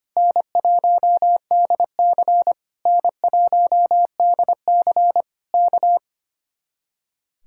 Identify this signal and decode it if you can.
Morse code, heard as a keyed tone, about 24 wpm, 700 Hz. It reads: N1DC N1DC K
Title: Speed-Racing Common QSO Phrases Encore - 25wpm